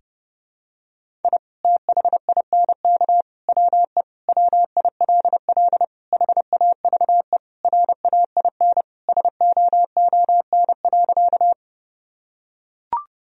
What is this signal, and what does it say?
Morse code, heard as a keyed tone, about 32 wpm, 700 Hz. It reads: I THINK WE WILL HAVE RAIN SOON.
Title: Sentences from Top 1000 Words - Review Part 1 - 30wpm